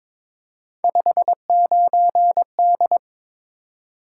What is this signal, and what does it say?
Morse code, heard as a keyed tone, about 22 wpm, 700 Hz. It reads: 59D
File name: practice / 22wpm / Three Letter-Number Combinations - 22wpm